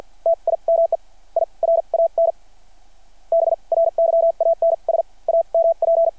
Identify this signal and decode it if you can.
Morse code, heard as a keyed tone, about 48 wpm, spaced as for 39 wpm, 660 Hz. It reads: TIME I RAN BRYANS AMP
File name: archived / CRSnet